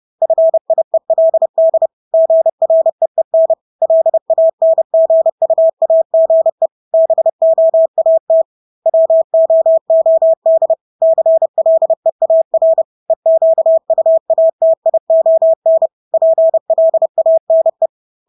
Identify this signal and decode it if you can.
Morse code, heard as a keyed tone, about 30 wpm, 650 Hz. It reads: FIELD GREEN LANGUAGE BOAT WOOD CLEAR EQUATION PLANE
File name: Engelske ord 30-30 wpm